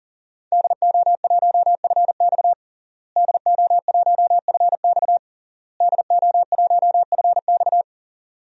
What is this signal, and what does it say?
Morse code, heard as a keyed tone, about 40 wpm, 700 Hz. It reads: DO1FX DO1FX DO1FX
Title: DXCC-100 Call Signs - 5 Characters - Repeated 3x - 40wpm